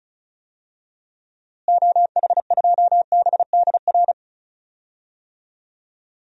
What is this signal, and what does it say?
Morse code, heard as a keyed tone, about 34 wpm, 700 Hz. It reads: OH2BDR
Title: DXCC-100 Call Signs - 6 Characters - Repeated 2x - 35wpm